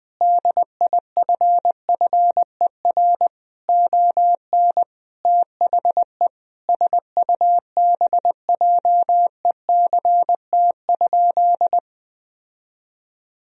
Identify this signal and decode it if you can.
Morse code, heard as a keyed tone, about 20 wpm, 700 Hz. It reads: DIFFER ON THE SUBJECT?
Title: Sentences from Top 1000 Words - Review Part 1 - 20wpm